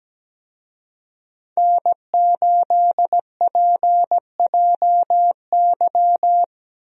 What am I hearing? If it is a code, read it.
Morse code, heard as a keyed tone, about 17 wpm, 700 Hz. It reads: N8PJY
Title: USA Call Signs - 5 Characters - Repeated 3x - 17wpm